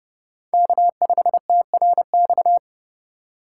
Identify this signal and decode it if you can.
Morse code, heard as a keyed tone, about 30 wpm, 700 Hz. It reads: K5TRX
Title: USA Call Signs - 5 Characters - 2x Character Spacing - 30wpm